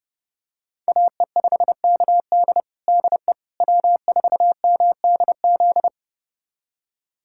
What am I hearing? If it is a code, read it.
Morse code, heard as a keyed tone, about 30 wpm, 700 Hz. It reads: AE5KD DE W4MDZ
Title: USA Callsign DE Callsign - 5 Characters - 30wpm